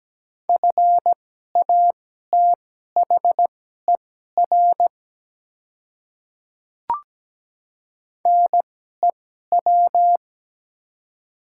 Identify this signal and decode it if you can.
Morse code, heard as a keyed tone, about 17 wpm, spaced as for 12 wpm, 700 Hz. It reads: FATHER NEW
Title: 2x Character Spacing - Top 200 Words - 17wpm